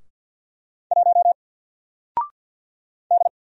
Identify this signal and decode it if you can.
Morse code, heard as a keyed tone, about 48 wpm, 700 Hz. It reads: 1 D
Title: Single Letter-Number Alt 50wpm